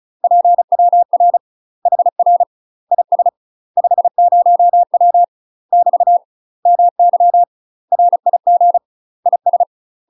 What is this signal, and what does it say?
Morse code, heard as a keyed tone, about 35 wpm, 700 Hz. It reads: PWR HR IS 50W <BT> MY RIG IS
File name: UK MASTER QSOs TRAINING/35WPM